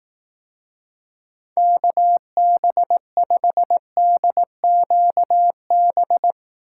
Morse code, heard as a keyed tone, about 18 wpm, 700 Hz. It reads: KB5DQB
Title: Speed Racing - USA Call Signs 15wpm